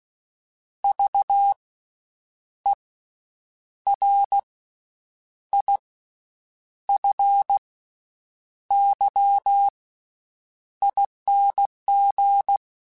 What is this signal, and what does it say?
Morse code, heard as a keyed tone, about 16 wpm, 800 Hz. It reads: V E R I F Y ING
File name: Special INGs-mp3s / Special INGs-mp3s / Slow ing @15